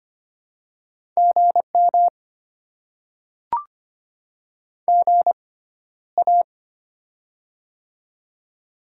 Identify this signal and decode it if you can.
Morse code, heard as a keyed tone, about 25 wpm, spaced as for 13 wpm, 690 Hz. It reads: GM GA
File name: Top 20 QSO Elements - 6x Character Spacing - 25wpm